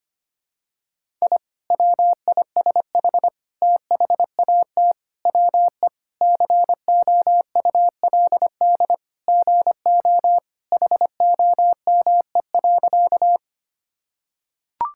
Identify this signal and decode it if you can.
Morse code, heard as a keyed tone, about 25 wpm, 700 Hz. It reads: I WISH THAT WE COULD GO HOME.
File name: practice / 25wpm / Sentences with 2-Word Phrases - 25wpm